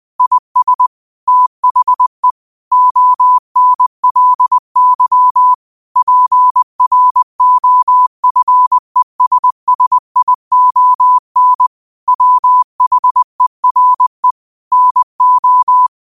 Morse code, heard as a keyed tone, about 20 wpm, 1000 Hz. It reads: IS THE ONLY PROFESSION WHERE NO